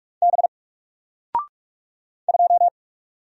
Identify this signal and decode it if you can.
Morse code, heard as a keyed tone, about 44 wpm, 700 Hz. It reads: B 2
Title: Single Character (Letter:Number:Punctuation:Prosign) 45wpm